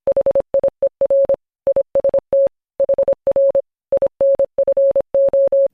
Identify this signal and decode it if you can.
Morse code, heard as a keyed tone, about 26 wpm, 550 Hz. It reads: HIER IST HR INFO